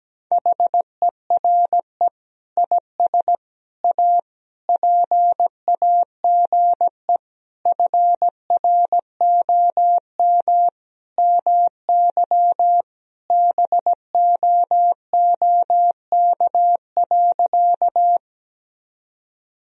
Morse code, heard as a keyed tone, about 17 wpm, 700 Hz. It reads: HERE IS A PAGE FROM MY BOOK.
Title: Sentences from Top 1000 Words - Review Part 1 - 17wpm